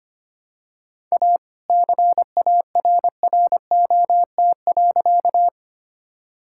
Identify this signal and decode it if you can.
Morse code, heard as a keyed tone, about 25 wpm, 700 Hz. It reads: A CARROT.
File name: practice / 25wpm / All-Age Appropriate Jokes - 25wpm